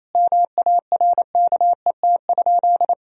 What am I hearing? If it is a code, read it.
Morse code, heard as a keyed tone, about 28 wpm, 700 Hz. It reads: MARKET?
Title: Sentences from Top 1000 Words - Review Part 3 - 28wpm